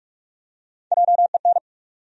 Morse code, heard as a keyed tone, about 46 wpm, 700 Hz. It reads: JEN
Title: USA Common Names - Max 3 Letters - 45wpm